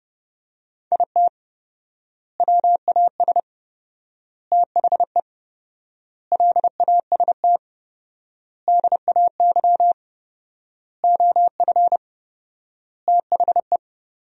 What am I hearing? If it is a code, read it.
Morse code, heard as a keyed tone, about 30 wpm, 700 Hz. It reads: IT WAS THE LAST DAY OF THE
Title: 4x Word-Spacing - Sentences from Top 300 Words - 30wpm